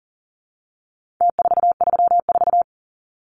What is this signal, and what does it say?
Morse code, heard as a keyed tone, about 40 wpm, 700 Hz. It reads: T434
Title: Numbers - Contest Style - 4-Digit 40wpm